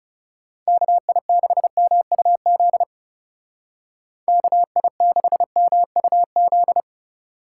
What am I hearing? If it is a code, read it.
Morse code, heard as a keyed tone, about 39 wpm, 700 Hz. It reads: KI6MUZ KI6MUZ